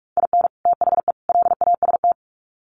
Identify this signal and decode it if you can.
Morse code, heard as a keyed tone, about 44 wpm, 700 Hz. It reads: IN THE LAST
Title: Sets of 3 Words 45wpm